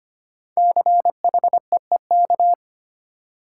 Morse code, heard as a keyed tone, about 25 wpm, 700 Hz. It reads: CHEEK
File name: 5 Letter Words - First 3 Letters Given - 25wpm